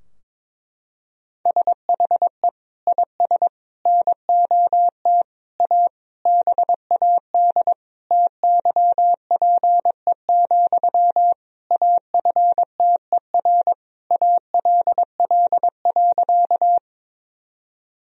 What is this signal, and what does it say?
Morse code, heard as a keyed tone, about 22 wpm, 700 Hz. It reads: SHE IS NOT A BAD TYPE, AFTER ALL.
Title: Sentences from Top 1000 Words - Review Part 3 - 22wpm